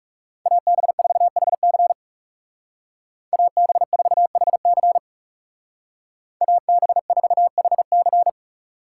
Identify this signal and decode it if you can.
Morse code, heard as a keyed tone, about 36 wpm, 700 Hz. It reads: AB4HC AB4HC AB4HC
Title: Speed Racing - USA 2x2 Call Signs 30wpm